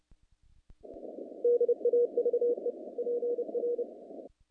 Morse code, heard as a keyed tone, about 30 wpm, 470 Hz. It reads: DAVE PR
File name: contests / naqp-cw-k5tr-op-k5pi- -2200